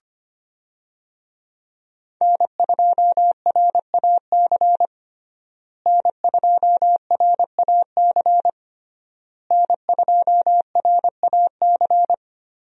Morse code, heard as a keyed tone, about 25 wpm, 700 Hz. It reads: N2RAC N2RAC N2RAC
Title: USA Call Signs - 5 Characters - Repeated 3x - 25wpm